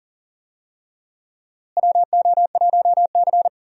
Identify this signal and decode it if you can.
Morse code, heard as a keyed tone, about 38 wpm, 700 Hz. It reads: WO1C
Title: USA Call Signs - 4 Characters - Repeated 3x - 40wpm